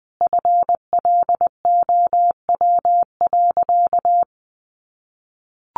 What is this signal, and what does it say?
Morse code, heard as a keyed tone, about 20 wpm, 700 Hz. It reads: FLOW.
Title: Sentences from Top 1200 Words 20wpm